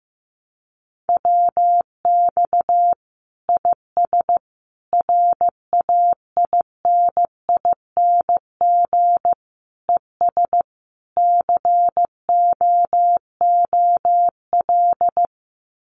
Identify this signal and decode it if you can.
Morse code, heard as a keyed tone, about 15 wpm, 700 Hz. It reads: WX IS RAINING ES COOL
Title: Common QSO Phrases 15wpm